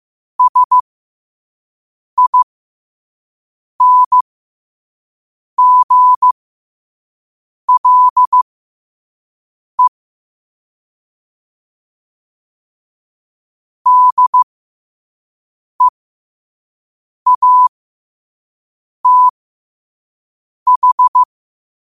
Morse code, heard as a keyed tone, about 15 wpm, spaced as for 5 wpm, 1000 Hz. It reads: SINGLE DEATH
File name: 5wpm